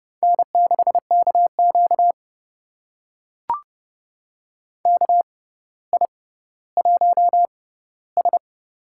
Morse code, heard as a keyed tone, about 31 wpm, spaced as for 16 wpm, 700 Hz. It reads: N6KQ KI1S
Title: USA Call Signs - 4 Characters - 6x Character Spacing - 30wpm